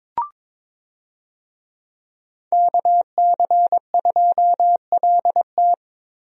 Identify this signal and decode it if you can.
Morse code, heard as a keyed tone, about 22 wpm, 700 Hz. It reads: KC2LT
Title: USA Call Signs - 5 Characters - Repeated 2x - 22wpm